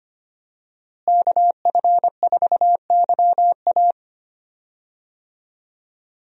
Morse code, heard as a keyed tone, about 25 wpm, 700 Hz. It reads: KF4YA
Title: USA Call Signs - 5 Characters - Repeated 3x - 25wpm